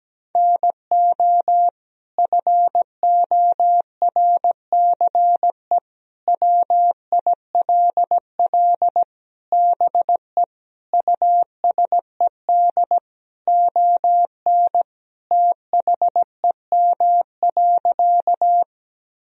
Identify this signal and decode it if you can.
Morse code, heard as a keyed tone, about 17 wpm, 700 Hz. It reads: NO FORCE WILL BE USED ON THEM.
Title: Sentences from Top 1000 Words - Review Part 3 - 17wpm